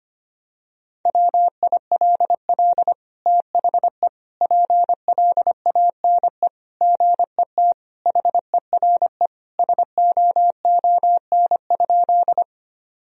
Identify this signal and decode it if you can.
Morse code, heard as a keyed tone, about 25 wpm, 700 Hz. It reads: WILL THE PLANE GET HERE SOON?
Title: Sentences from Top 1000 Words - Review Part 3 - 25wpm